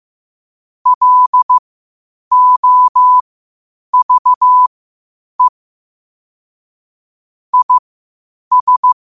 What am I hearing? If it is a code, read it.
Morse code, heard as a keyed tone, about 15 wpm, spaced as for 8 wpm, 1000 Hz. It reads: LOVE IS
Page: Quotes for Fri, 15 Aug 2025 in Morse Code at 8 words per minute.